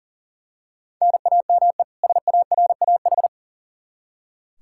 Morse code, heard as a keyed tone, about 40 wpm, 700 Hz. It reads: NAME SARAH
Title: K1USN SST QSO Practice 40wpm